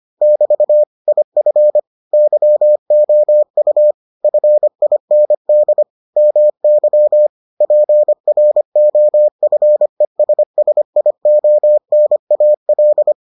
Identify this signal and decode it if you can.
Morse code, heard as a keyed tone, about 25 wpm, 600 Hz. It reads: <BT> IF YOU FIND MY PROFESSIONAL